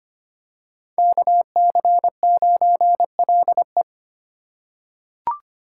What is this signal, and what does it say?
Morse code, heard as a keyed tone, about 25 wpm, 700 Hz. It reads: KC9LE
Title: USA Call Signs - 3x Character Spacing - 25wpm